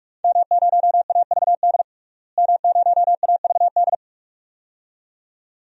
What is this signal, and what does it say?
Morse code, heard as a keyed tone, about 46 wpm, 700 Hz. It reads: M0AVD M0AVD
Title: DXCC-100 Call Signs - 5 Characters - Repeated 2x - 45wpm